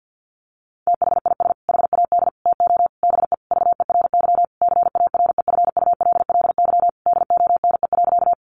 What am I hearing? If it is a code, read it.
Morse code, heard as a keyed tone, about 50 wpm, 700 Hz. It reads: THIS HAD TO BE VERY CAREFULLY DONE.
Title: Sentences from Top 1300 Words 50wpm